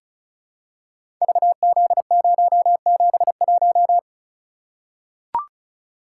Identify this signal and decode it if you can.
Morse code, heard as a keyed tone, about 36 wpm, 700 Hz. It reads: VZ071
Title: Code Groups - 5 Letter:Numbers - 35wpm